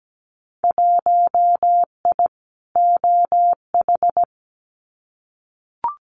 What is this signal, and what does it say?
Morse code, heard as a keyed tone, about 17 wpm, 700 Hz. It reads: 1I OH
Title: Winter Field Day Exchanges 17wpm